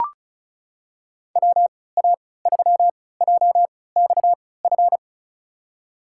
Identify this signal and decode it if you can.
Morse code, heard as a keyed tone, about 34 wpm, spaced as for 20 wpm, 700 Hz. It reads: WA3JXF
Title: USA Call Signs - 6 Characters - 3x Character Spacing - 35wpm